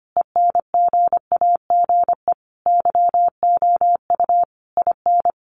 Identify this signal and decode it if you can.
Morse code, heard as a keyed tone, about 25 wpm, 700 Hz. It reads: ENGAGE YOU IN
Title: Sentences from Top 1500 Words 25wpm